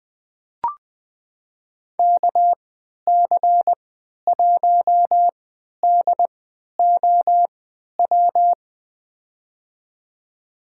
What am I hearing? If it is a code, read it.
Morse code, heard as a keyed tone, about 20 wpm, spaced as for 11 wpm, 700 Hz. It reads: KC1DOW
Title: USA Call Signs - 6 Characters - 3x Character Spacing - 20wpm